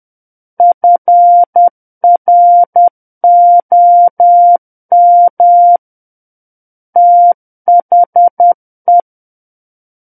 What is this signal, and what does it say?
Morse code, heard as a keyed tone, about 10 wpm, 700 Hz. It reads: FROM THE
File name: Book One / 01-Chapter 1 - 10 WPM